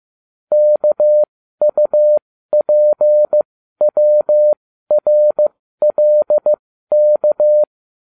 Morse code, heard as a keyed tone, about 15 wpm, spaced as for 12 wpm, 600 Hz. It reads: KUPWRLK